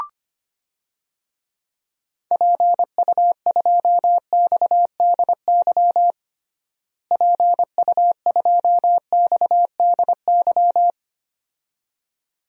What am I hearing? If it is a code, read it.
Morse code, heard as a keyed tone, about 25 wpm, 700 Hz. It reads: PU2XDY PU2XDY
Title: DXCC-100 Call Signs - 6 Characters - Repeated 2x - 25wpm